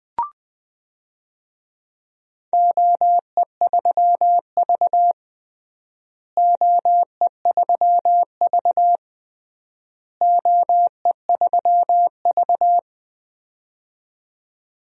Morse code, heard as a keyed tone, about 20 wpm, 700 Hz. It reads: OE3V OE3V OE3V
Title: DXCC-100 Call Signs - 4 Characters - Repeated 3x - 20wpm